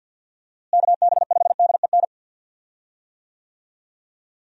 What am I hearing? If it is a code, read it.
Morse code, heard as a keyed tone, about 51 wpm, 700 Hz. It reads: KB5DEN
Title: USA Call Signs - 6 Characters - Repeated 2x - 50wpm